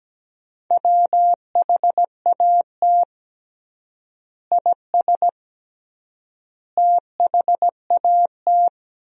Morse code, heard as a keyed tone, about 17 wpm, 700 Hz. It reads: WHAT IS THAT
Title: 3x Word-Spacing - Sentences from Top 200 Words - 17wpm